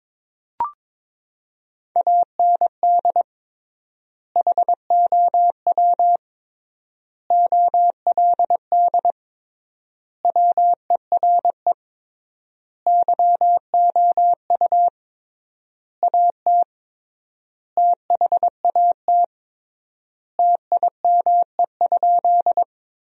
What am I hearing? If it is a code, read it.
Morse code, heard as a keyed tone, about 22 wpm, 700 Hz. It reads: AND HOW OLD WERE YOU AT THAT TIME?
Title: 3x Word-Spacing - Sentences from Top 200 Words - 22wpm